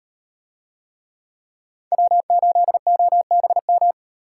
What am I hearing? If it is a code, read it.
Morse code, heard as a keyed tone, about 38 wpm, 700 Hz. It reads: W8OBM